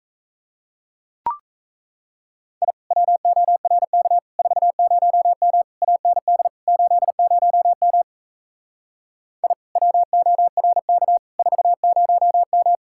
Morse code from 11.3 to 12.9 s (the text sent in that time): 40M